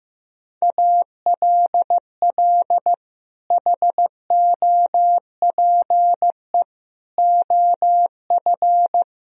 Morse code, heard as a keyed tone, about 15 wpm, 700 Hz. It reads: ALL HOPE OF
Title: Sentences from Top 1000 Words - Review Part 4 - 15wpm